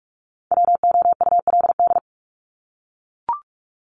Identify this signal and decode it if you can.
Morse code, heard as a keyed tone, about 44 wpm, 700 Hz. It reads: WOULD
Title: Top 200 Words 45wpm